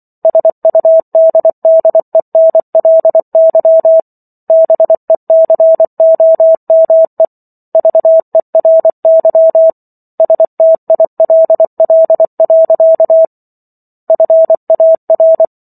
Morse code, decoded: SUDDENLY BECOME VERY STILL. FAR

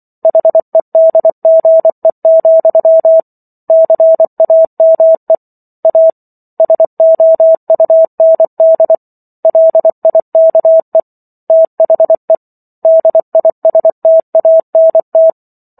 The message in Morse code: HEDGE, CAME A SOUND LIKE THE DISTANT

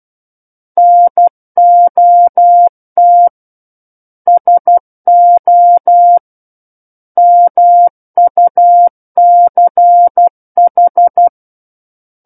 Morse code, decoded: NOT SO MUCH

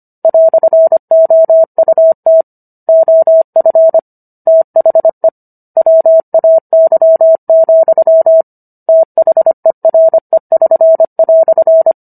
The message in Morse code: "OUT OF THE WAY, THERE<SN>"

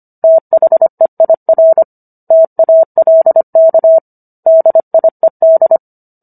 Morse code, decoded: THEIR TALK DIED

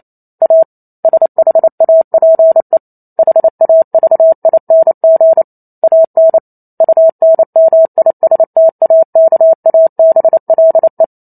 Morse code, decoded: A SHAPE HAVING AN UNMISTAKABLE

